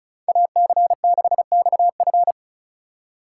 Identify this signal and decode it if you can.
Morse code, heard as a keyed tone, about 36 wpm, 700 Hz. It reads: AC6XF
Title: USA Call Signs - 4x Character Spacing - 35wpm